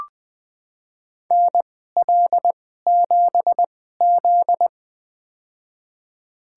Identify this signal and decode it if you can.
Morse code, heard as a keyed tone, about 20 wpm, spaced as for 15 wpm, 700 Hz. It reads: NL7Z
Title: USA Call Signs - 4 Characters - 2x Character Spacing - 20wpm